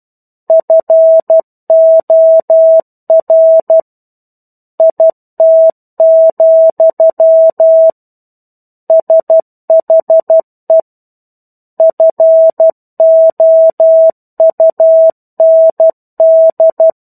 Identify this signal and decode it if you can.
Morse code, heard as a keyed tone, about 12 wpm, 650 Hz. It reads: FOR IT, SHE FOUND
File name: 01-Chapter-01 12 WPM